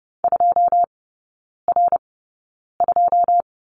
Morse code, heard as a keyed tone, about 30 wpm, spaced as for 9 wpm, 700 Hz. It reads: 2R2
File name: 081 Lesson 14e - Number 2 - Three Characters 30wpm